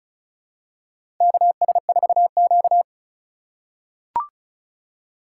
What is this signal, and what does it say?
Morse code, heard as a keyed tone, about 34 wpm, 700 Hz. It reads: KS4Q
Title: USA Call Signs - 4 Characters - 3x Character Spacing - 35wpm